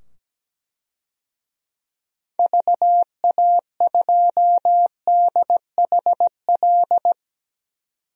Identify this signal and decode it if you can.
Morse code, heard as a keyed tone, about 17 wpm, 700 Hz. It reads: VA2DHL
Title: DXCC-100 Call Signs - 6 Characters - 17wpm